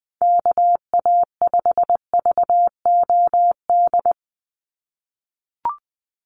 Morse code, heard as a keyed tone, about 20 wpm, 700 Hz. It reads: KA5VOD
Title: USA Call Signs 20wpm